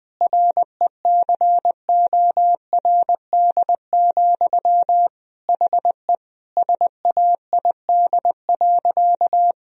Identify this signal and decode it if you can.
Morse code, heard as a keyed tone, about 20 wpm, 700 Hz. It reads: RECORD, HE SAID.
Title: Sentences from Top 1000 Words - Review Part 3 - 20wpm